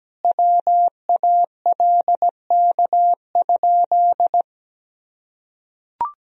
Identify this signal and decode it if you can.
Morse code, heard as a keyed tone, about 17 wpm, 700 Hz. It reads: WALK?
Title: Sentences from Top 1000 Words - Review Part 1 - 17wpm